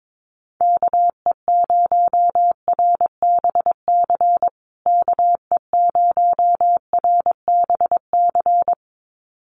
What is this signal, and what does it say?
Morse code, heard as a keyed tone, about 22 wpm, 700 Hz. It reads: KE0RBC KE0RBC
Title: USA Call Signs Repeated 2x 22wpm